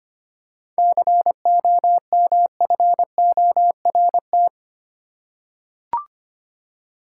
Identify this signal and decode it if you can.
Morse code, heard as a keyed tone, about 25 wpm, 700 Hz. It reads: COMFORT
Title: Seven Letter Words - 25wpm